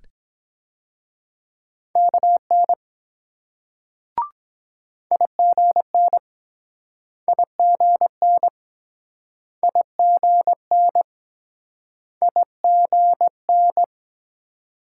Morse code, heard as a keyed tone, about 24 wpm, 700 Hz. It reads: KN IGN IGN IGN IGN